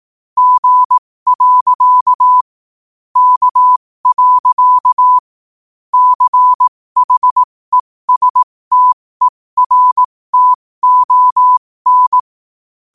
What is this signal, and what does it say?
Morse code, heard as a keyed tone, about 18 wpm, spaced as for 15 wpm, 1000 Hz. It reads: G. K. CHESTERTON
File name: qotdfiles / 15wpm